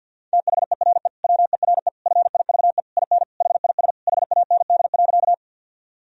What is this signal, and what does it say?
Morse code, heard as a keyed tone, about 50 wpm, 700 Hz. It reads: THERE WERE FIVE IN HIS HAND.
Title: Sentences from Top 1000 Words - Review Part 2 - 50wpm